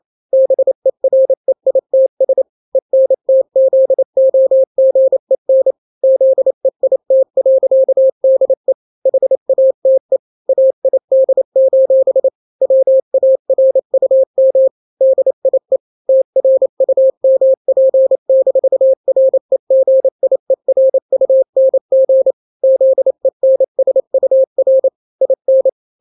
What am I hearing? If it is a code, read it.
Morse code, heard as a keyed tone, about 28 wpm, 530 Hz. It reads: BEREITS ENTZOGEN ZEIT.DE HATE AID: WARUM DIE TRUMP-REGIERUNG ZENSUR IN
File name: daily / 27wpm